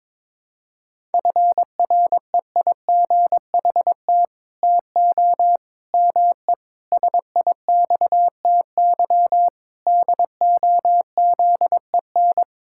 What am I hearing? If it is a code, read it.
Morse code, heard as a keyed tone, about 22 wpm, 700 Hz. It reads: FREIGHT TO ME SIXTY DOZEN
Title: Pangram Challenge v2 - 22wpm